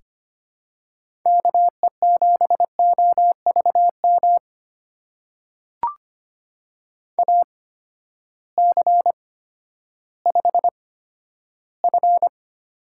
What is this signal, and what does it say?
Morse code, heard as a keyed tone, about 25 wpm, spaced as for 12 wpm, 700 Hz. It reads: KE7OVM AC5F
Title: USA Call Signs - 8x Character Spacing - 25wpm